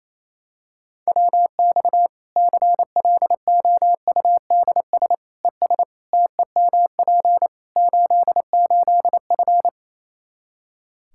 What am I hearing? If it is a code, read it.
Morse code, heard as a keyed tone, about 28 wpm, 700 Hz. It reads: WX CLOUDS ES TEMP 88F
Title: K1USN SST QSO Practice 28wpm